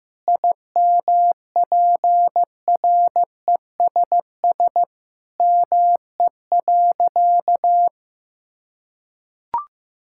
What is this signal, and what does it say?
Morse code, heard as a keyed tone, about 15 wpm, 700 Hz. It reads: IMPRESS ME.
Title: Sentences from Top 1600 Words 15wpm